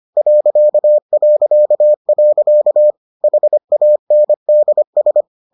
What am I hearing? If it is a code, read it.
Morse code, heard as a keyed tone, about 25 wpm, 600 Hz. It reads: ... HANDS